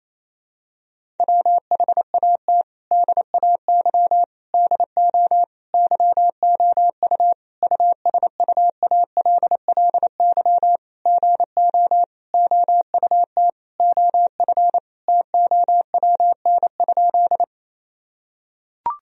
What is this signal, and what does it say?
Morse code, heard as a keyed tone, about 28 wpm, 700 Hz. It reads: WHAT DAY DO YOU USUALLY GO OUT OF TOWN?
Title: Sentences from Top 1000 Words - Review Part 3 - 28wpm